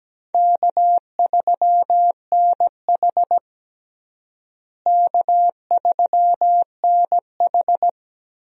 Morse code, heard as a keyed tone, about 17 wpm, 700 Hz. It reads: K3NH K3NH
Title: USA Call Signs - 4 Characters - Repeated 3x - 17wpm